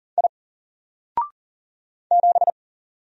Morse code, heard as a keyed tone, about 40 wpm, 700 Hz. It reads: I 7
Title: Single Character (Letter:Number:Punctuation:Prosign) Alt 40wpm